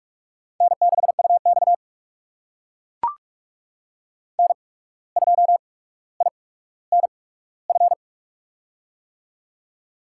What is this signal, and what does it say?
Morse code, heard as a keyed tone, about 44 wpm, spaced as for 18 wpm, 700 Hz. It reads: N6UX N2INF
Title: USA Call Signs - 8x Character Spacing - 45wpm